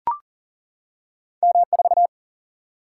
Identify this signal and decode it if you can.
Morse code, heard as a keyed tone, about 39 wpm, 700 Hz. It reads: M4
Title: Single-Letter Single-Number 40wpm